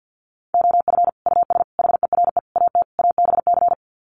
Morse code, heard as a keyed tone, about 50 wpm, 700 Hz. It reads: OF US HERE AT ABC
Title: Sets of 5 Words 50wpm